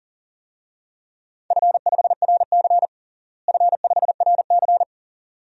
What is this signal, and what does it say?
Morse code, heard as a keyed tone, about 37 wpm, 700 Hz. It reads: F5RC F5RC
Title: DXCC-100 Call Signs - 4 Characters - Repeated 2x - 40wpm